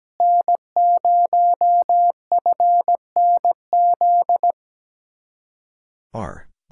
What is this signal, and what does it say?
Morse code, heard as a keyed tone, about 17 wpm, 700 Hz. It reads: N0FNZ
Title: Speed-Racing Common QSO Phrases - 17wpm